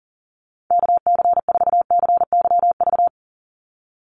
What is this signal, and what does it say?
Morse code, heard as a keyed tone, about 40 wpm, 700 Hz. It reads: KC4CYV
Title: USA Call Signs 40wpm